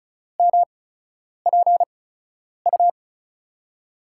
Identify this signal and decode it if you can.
Morse code, heard as a keyed tone, about 36 wpm, spaced as for 10 wpm, 700 Hz. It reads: MPU
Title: USA Call Signs - 5 Characters - 8x Character Spacing - 35wpm